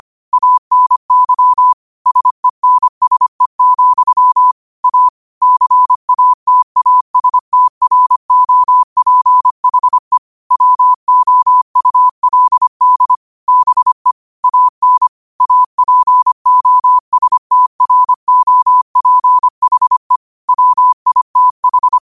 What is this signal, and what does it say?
Morse code, heard as a keyed tone, about 25 wpm, 1000 Hz. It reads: ANY SENSE, A CATASTROPHE WOULD BE AN APOSTROPHE WITH